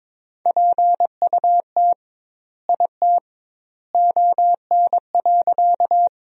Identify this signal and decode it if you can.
Morse code, heard as a keyed tone, about 22 wpm, 700 Hz. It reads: PUT IT ON.
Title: 2x Word-Spacing - Sentences from Top 200 Words - 22wpm